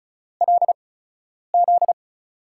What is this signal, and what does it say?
Morse code, heard as a keyed tone, about 36 wpm, spaced as for 10 wpm, 700 Hz. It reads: LZ
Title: USA Call Signs - 8x Character Spacing - 35wpm